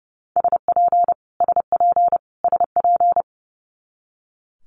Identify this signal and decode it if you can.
Morse code, heard as a keyed tone, about 30 wpm, 700 Hz. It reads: SP SP SP
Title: Multi-letter Phonemes Repeated 3x 30wpm